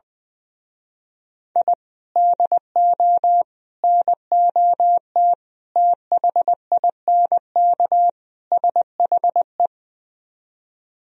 Morse code, heard as a keyed tone, about 20 wpm, 700 Hz. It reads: I DO NOT THINK SHE
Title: Speed-Racing - Sets of 5 Words 20wpm